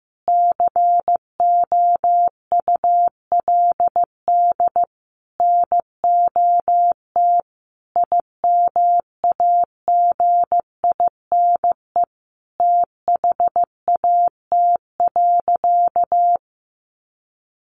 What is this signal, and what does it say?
Morse code, heard as a keyed tone, about 15 wpm, 700 Hz. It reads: COULD NOT IMAGINE THAT.
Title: Sentences from Top 1400 Words 15wpm